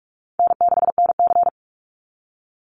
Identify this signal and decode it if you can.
Morse code, heard as a keyed tone, about 44 wpm, 700 Hz. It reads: N6NC
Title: Easy Call Signs Repeated 3x - 45wpm